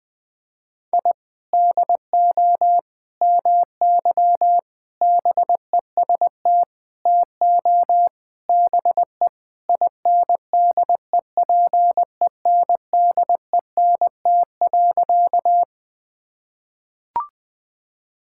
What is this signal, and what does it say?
Morse code, heard as a keyed tone, about 20 wpm, 700 Hz. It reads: I DO MY BEST TO BE INDEPENDENT.
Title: Sentences from Top 1000 Words - Review Part 6 - 20wpm